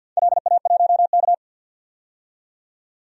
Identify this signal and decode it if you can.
Morse code, heard as a keyed tone, about 53 wpm, 700 Hz. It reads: LA1K
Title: DXCC-100 Call Signs - 4 Characters - 50wpm